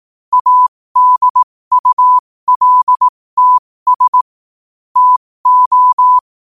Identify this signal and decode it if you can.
Morse code, heard as a keyed tone, about 18 wpm, spaced as for 15 wpm, 1000 Hz. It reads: ADULTS TO
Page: New quotes every day in morse code at 15 Words per minute.